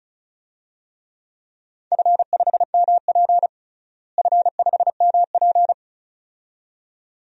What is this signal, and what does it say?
Morse code, heard as a keyed tone, about 36 wpm, 700 Hz. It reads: F5MP F5MP
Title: DXCC-100 Call Signs - 4 Characters - Repeated 2x - 35wpm